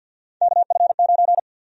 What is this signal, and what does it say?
Morse code, heard as a keyed tone, about 46 wpm, 700 Hz. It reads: KF9
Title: USA Call Sign Prefixes - 50wpm